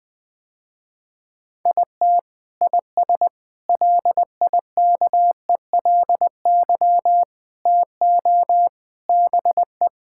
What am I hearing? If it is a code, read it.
Morse code, heard as a keyed tone, about 20 wpm, 700 Hz. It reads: IT IS LIKELY TO BE
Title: Sentences with 3-Word Phrases - Spoken Beginning - 20wpm